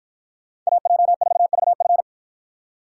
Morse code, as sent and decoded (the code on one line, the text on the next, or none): .- .--- ....- ...- ..-.
AJ4VF